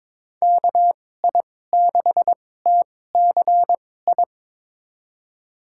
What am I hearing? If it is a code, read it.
Morse code, heard as a keyed tone, about 22 wpm, spaced as for 16 wpm, 700 Hz. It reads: KI6TCI
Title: USA Call Signs - 6 Characters - 2x Character Spacing - 22wpm